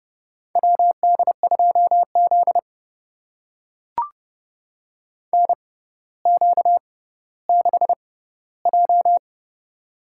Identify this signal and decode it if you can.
Morse code, heard as a keyed tone, about 32 wpm, spaced as for 16 wpm, 700 Hz. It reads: WD2Z NQ6J
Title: USA Call Signs - 4 Characters - 6x Character Spacing - 30wpm